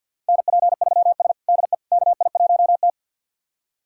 Morse code, heard as a keyed tone, about 50 wpm, 700 Hz. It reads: NP3S DE KI1T
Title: USA Callsign DE Callsign - 4 Characters - 50wpm